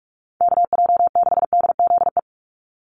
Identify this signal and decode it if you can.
Morse code, heard as a keyed tone, about 44 wpm, 700 Hz. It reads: KJ6DZE
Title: USA Call Signs 45wpm